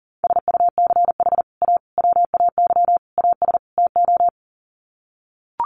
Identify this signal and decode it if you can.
Morse code, heard as a keyed tone, about 40 wpm, 700 Hz. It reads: SUCH A WAY AS TO
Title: Sets of 5 Words Encore2 40wpm